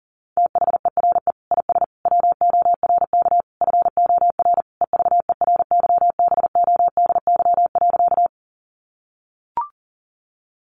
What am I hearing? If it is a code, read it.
Morse code, heard as a keyed tone, about 40 wpm, 700 Hz. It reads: THERE IS WORK FOR EVERYBODY.
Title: Sentences from Top 1400 Words 40wpm